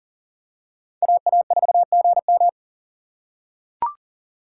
Morse code, heard as a keyed tone, about 40 wpm, 700 Hz. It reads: AA4GM
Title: USA Call Signs - 5 Characters - 3x Character Spacing - 40wpm